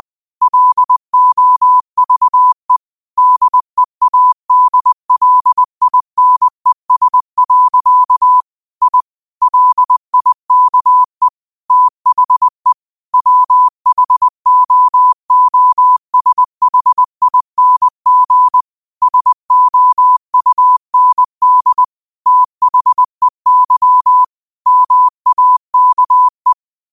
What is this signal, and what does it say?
Morse code, heard as a keyed tone, about 20 wpm, 1000 Hz. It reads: LOVE DEADLINES. I LIKE THE WHOOSHING SOUND THEY MAKE